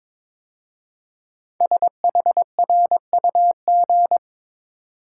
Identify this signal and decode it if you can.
Morse code, heard as a keyed tone, about 22 wpm, 700 Hz. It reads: SHRUG
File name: 5 Letter Words - First 3 Letters Given - 22wpm